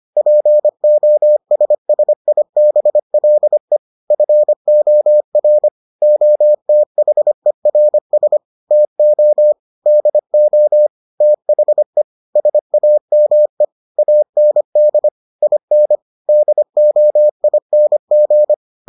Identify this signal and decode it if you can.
Morse code, heard as a keyed tone, about 25 wpm, 600 Hz. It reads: POSSIBLE FOR OTHERS TO DO THE SAME AND IN DOING